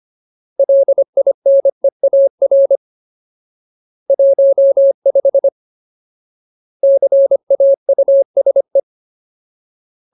Morse code, heard as a keyed tone, about 25 wpm, 550 Hz. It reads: LINEAR 15 CAUSE
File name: Ultimate Word List / 25WPM Ultimate